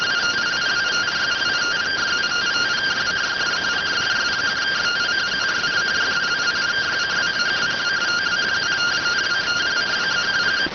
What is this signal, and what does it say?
Two-tone teleprinter signal, about 170 Hz shift.